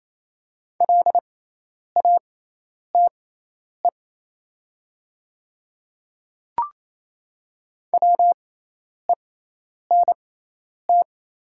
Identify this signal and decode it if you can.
Morse code, heard as a keyed tone, about 28 wpm, spaced as for 10 wpm, 700 Hz. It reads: LATE WENT
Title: 6x Character Spacing - Top 500 Words - 28wpm